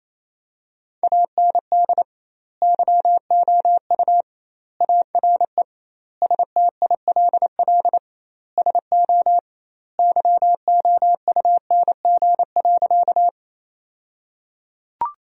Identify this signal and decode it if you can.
Morse code, heard as a keyed tone, about 28 wpm, 700 Hz. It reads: AND YOU ARE STILL SO YOUNG.
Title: 2x Word-Spacing - Sentences from Top 400 Words - 28wpm